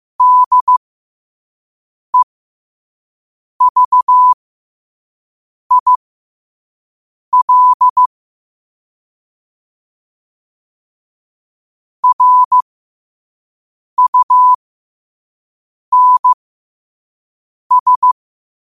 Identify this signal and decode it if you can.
Morse code, heard as a keyed tone, about 15 wpm, spaced as for 5 wpm, 1000 Hz. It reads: DEVIL RUNS
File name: qotdfiles / 5wpm